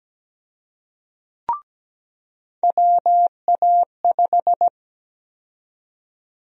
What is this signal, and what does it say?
Morse code, heard as a keyed tone, about 17 wpm, 700 Hz. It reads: WA5
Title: USA Call Sign Prefixes - 17wpm